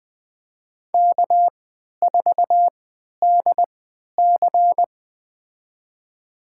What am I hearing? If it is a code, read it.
Morse code, heard as a keyed tone, about 20 wpm, spaced as for 11 wpm, 700 Hz. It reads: K4DC
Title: USA Call Signs - 4 Characters - 3x Character Spacing - 20wpm